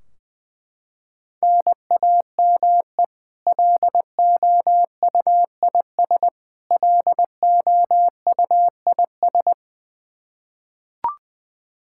Morse code, heard as a keyed tone, about 20 wpm, 700 Hz. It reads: NAME LOUIS LOUIS
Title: K1USN SST QSO Practice 20wpm